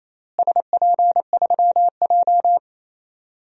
Morse code, heard as a keyed tone, about 28 wpm, 700 Hz. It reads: SP3J
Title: DXCC-100 Call Signs - 4 Characters - Repeated 3x - 28wpm